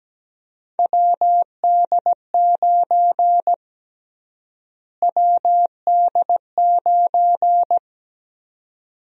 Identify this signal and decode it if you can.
Morse code, heard as a keyed tone, about 17 wpm, 700 Hz. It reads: WD9 WD9
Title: USA Call Sign Prefixes - Repeated 2x - 17wpm